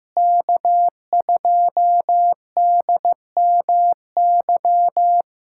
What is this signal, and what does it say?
Morse code, heard as a keyed tone, about 15 wpm, 700 Hz. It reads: K2DMY
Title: USA Callsign DE Callsign - 5 Characters - 15wpm